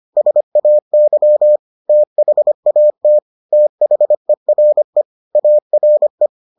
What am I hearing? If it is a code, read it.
Morse code, heard as a keyed tone, about 25 wpm, 600 Hz. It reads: SAY THAT THERE ARE